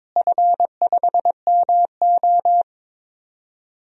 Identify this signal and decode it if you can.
Morse code, heard as a keyed tone, about 22 wpm, 700 Hz. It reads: F5MO
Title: DXCC-100 Call Signs - 4 Characters - Repeated 2x - 22wpm